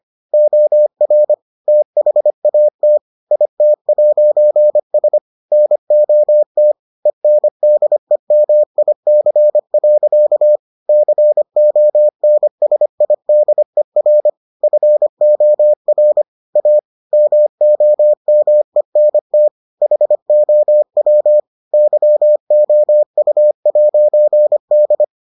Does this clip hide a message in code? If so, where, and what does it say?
Yes: Morse code from 0.3 to 25.0 s, heard as a keyed tone, about 25 wpm, 600 Hz. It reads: OR THAT IT'S NOT ENDEMIC. CONSIDER FOR A MOMENT HOW YOU'D